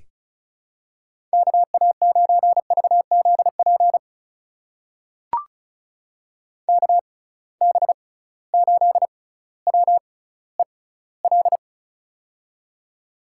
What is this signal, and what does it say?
Morse code, heard as a keyed tone, about 36 wpm, spaced as for 18 wpm, 700 Hz. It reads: KA9VZP KB8WEL
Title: USA Call Signs - 6 Characters - 6x Character Spacing - 35wpm